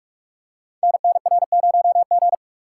Morse code, heard as a keyed tone, about 44 wpm, 700 Hz. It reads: NNR0G
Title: Code Groups - 5 Letter:Numbers - 45wpm